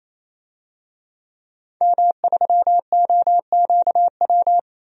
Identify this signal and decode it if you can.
Morse code, heard as a keyed tone, about 27 wpm, 700 Hz. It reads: M3OQW
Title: DXCC-100 Call Signs - 5 Characters - 28wpm